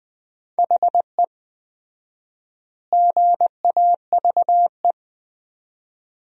Morse code, heard as a keyed tone, about 21 wpm, 700 Hz. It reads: HE GAVE
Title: 4x Word-Spacing - Sentences from Top 500 Words - 20wpm